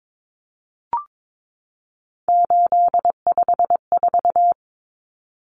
Morse code, heard as a keyed tone, about 22 wpm, 700 Hz. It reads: 854
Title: Common Cut Numbers - 3-Digit 22wpm